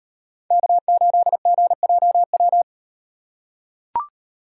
Morse code, heard as a keyed tone, about 35 wpm, 700 Hz. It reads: K8GJW